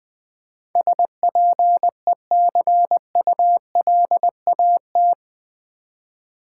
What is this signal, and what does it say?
Morse code, heard as a keyed tone, about 19 wpm, 700 Hz. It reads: SPECULAT
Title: Build A Word - 4 Letters At A Time - 11-Letter Words - 20wpm